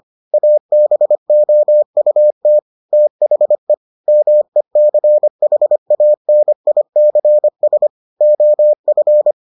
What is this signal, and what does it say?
Morse code, heard as a keyed tone, about 25 wpm, 600 Hz. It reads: ABOUT THE MECHANICS OF